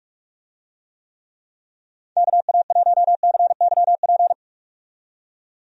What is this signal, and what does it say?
Morse code, heard as a keyed tone, about 46 wpm, 700 Hz. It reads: KA1CYP
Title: USA Call Signs - 6 Characters - 45wpm